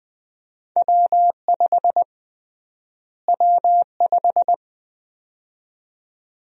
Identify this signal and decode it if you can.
Morse code, heard as a keyed tone, about 21 wpm, 700 Hz. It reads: W5 W5
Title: USA Call Sign Prefixes - Repeated 2x - 20wpm